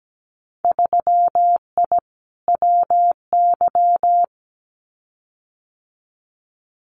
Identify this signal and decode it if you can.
Morse code, heard as a keyed tone, about 17 wpm, 700 Hz. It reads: 3I WY
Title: Winter Field Day Exchanges Encore 17wpm